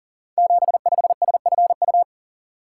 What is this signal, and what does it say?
Morse code, heard as a keyed tone, about 42 wpm, 700 Hz. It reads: 75SFU
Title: Code Groups - 5 Letter:Numbers - 40wpm